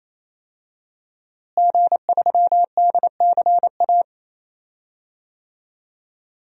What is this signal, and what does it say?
Morse code, heard as a keyed tone, about 28 wpm, 700 Hz. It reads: G3DCA
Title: DXCC-100 Call Signs - 5 Characters - Repeated 3x - 28wpm